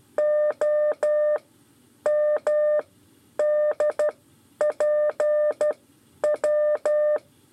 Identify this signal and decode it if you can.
Morse code, heard as a keyed tone, about 12 wpm, spaced as for 9 wpm, 600 Hz. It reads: OMDPW
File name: 5 LEZIONE RIPASSO 2